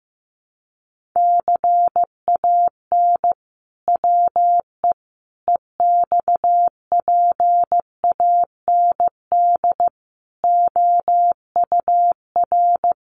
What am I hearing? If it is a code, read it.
Morse code, heard as a keyed tone, about 15 wpm, 700 Hz. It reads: CAN WE EXPAND OUR
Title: Sentences from Top 1400 Words 15wpm